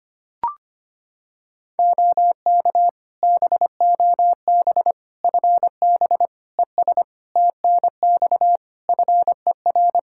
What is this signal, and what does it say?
Morse code, heard as a keyed tone, about 25 wpm, 700 Hz. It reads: OK BOB FB ES TNX FER
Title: Practice QSO - Exchange 2 - Info - 25wpm